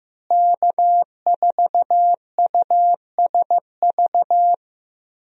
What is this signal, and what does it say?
Morse code, heard as a keyed tone, about 15 wpm, 700 Hz. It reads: K4USV
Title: USA Call Signs - 5 Characters - Repeated 2x - 15wpm